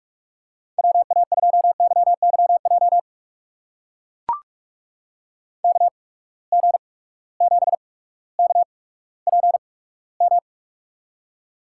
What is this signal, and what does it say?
Morse code, heard as a keyed tone, about 44 wpm, spaced as for 19 wpm, 700 Hz. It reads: WA2YYJ KG7KPM
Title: USA Call Signs - 6 Characters - 8x Character Spacing - 45wpm